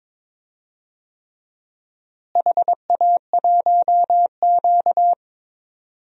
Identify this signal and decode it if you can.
Morse code, heard as a keyed tone, about 22 wpm, 700 Hz. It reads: HA1Q
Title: DXCC-100 Call Signs - 4 Characters - Repeated 3x - 22wpm